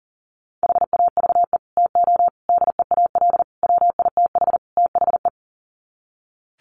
Morse code, heard as a keyed tone, about 40 wpm, 700 Hz. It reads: HAVE TO DEAL WITH THE
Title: Sets of 5 Words Encore2 40wpm